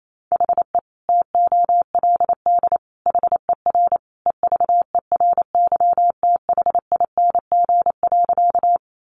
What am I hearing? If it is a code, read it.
Morse code, heard as a keyed tone, about 28 wpm, 700 Hz. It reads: HE TOLD HER EVERYTHING.
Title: Sentences from Top 700 Words 28wpm